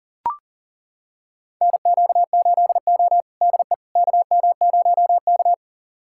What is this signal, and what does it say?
Morse code, heard as a keyed tone, about 40 wpm, 700 Hz. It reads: NQ8O DE KM0K
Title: USA Callsign DE Callsign - 4 Characters - 40wpm